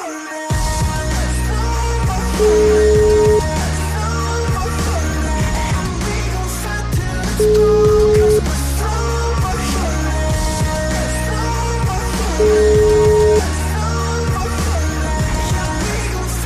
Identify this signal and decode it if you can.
Ringback tone (Europe, 425 Hz), with noise.